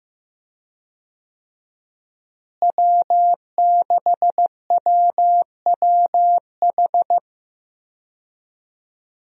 Morse code, heard as a keyed tone, about 15 wpm, 700 Hz. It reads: W6WWH
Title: USA Call Signs - 5 Characters - Repeated 3x - 15wpm